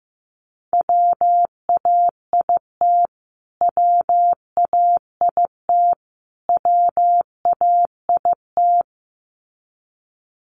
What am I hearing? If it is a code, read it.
Morse code, heard as a keyed tone, about 15 wpm, 700 Hz. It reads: WAIT WAIT WAIT
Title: Top 500 Words Repeated 15wpm